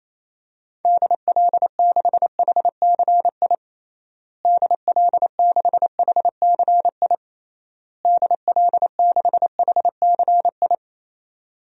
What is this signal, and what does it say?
Morse code, heard as a keyed tone, about 28 wpm, 700 Hz. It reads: DL6HCI DL6HCI DL6HCI
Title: DXCC-100 Call Signs - 6 Characters - Repeated 3x - 28wpm